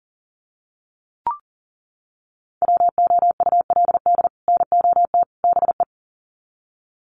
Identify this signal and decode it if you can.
Morse code, heard as a keyed tone, about 40 wpm, 700 Hz. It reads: WOULD NOT BE
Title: Sets of 3 Words 40wpm